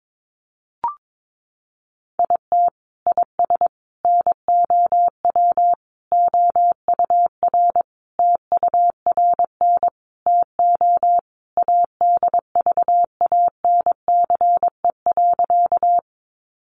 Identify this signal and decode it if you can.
Morse code, heard as a keyed tone, about 22 wpm, 700 Hz. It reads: IT IS NOW OUR TURN TO ADVANCE.
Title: Sentences from Top 1400 Words 22wpm